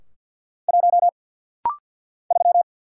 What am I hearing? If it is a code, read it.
Morse code, heard as a keyed tone, about 49 wpm, 700 Hz. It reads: 1 3
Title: Single Letter-Number - Rapid-Fire 50wpm